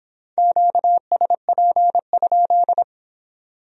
Morse code, heard as a keyed tone, about 25 wpm, 700 Hz. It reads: QSP?